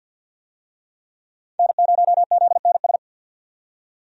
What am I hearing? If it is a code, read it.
Morse code, heard as a keyed tone, about 50 wpm, 700 Hz. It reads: N0ZNS
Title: USA Call Signs - 5 Characters - 50wpm